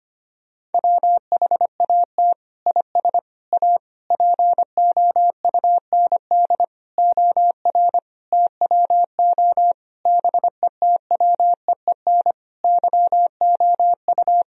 Morse code, heard as a keyed tone, about 25 wpm, 700 Hz. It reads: WHAT IS A POUND OR TWO BETWEEN YOU
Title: Sentences from Top 1000 Words - Review Part 2 - 25wpm